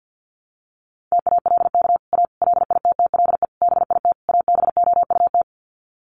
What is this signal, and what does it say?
Morse code, heard as a keyed tone, about 50 wpm, 700 Hz. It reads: TALK A LITTLE BIT ABOUT
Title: Sets of 5 Words Encore 50wpm